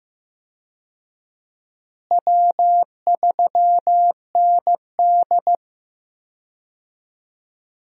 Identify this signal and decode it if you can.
Morse code, heard as a keyed tone, about 15 wpm, 700 Hz. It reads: W3ND
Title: USA Call Signs - 4 Characters - Repeated 3x - 15wpm